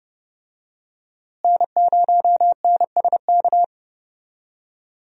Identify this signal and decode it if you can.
Morse code, heard as a keyed tone, about 32 wpm, 700 Hz. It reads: N0NSK
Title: USA Call Signs - 5 Characters - Repeated 2x - 30wpm